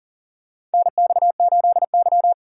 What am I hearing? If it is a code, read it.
Morse code, heard as a keyed tone, about 37 wpm, 700 Hz. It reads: NX8Y
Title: USA Call Signs - 6x Character Spacing - 40wpm